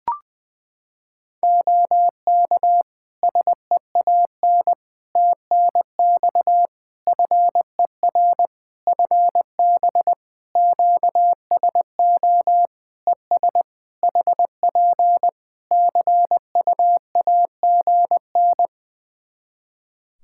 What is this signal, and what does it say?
Morse code, heard as a keyed tone, about 21 wpm, 700 Hz. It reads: OK SEAN TNX FER FB QSO ES HP CUAGN
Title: Practice QSO - Exchange 4 - Ending - 20wpm